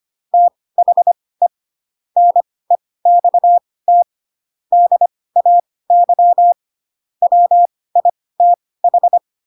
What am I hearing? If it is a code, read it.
Morse code, heard as a keyed tone, about 25 wpm, spaced as for 18 wpm, 700 Hz. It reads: THE NEXT DAY WITH
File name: Pinocchio / Pinocchio ch6 12 25 ik1hge